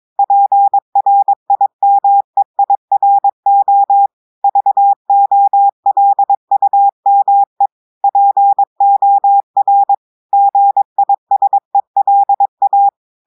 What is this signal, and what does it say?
Morse code, heard as a keyed tone, about 22 wpm, 800 Hz. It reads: PRIMEIRO VOLUME POR GISELA
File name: audios / portugais / portugais texte en clair 1 vitesse 22 mots minutes